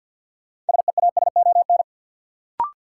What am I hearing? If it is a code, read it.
Morse code, heard as a keyed tone, about 47 wpm, 700 Hz. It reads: SEASON
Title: Sentences from Top 1400 Words - New Word Drill - 50wpm